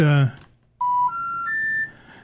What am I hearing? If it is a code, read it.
Special information tone, with speech.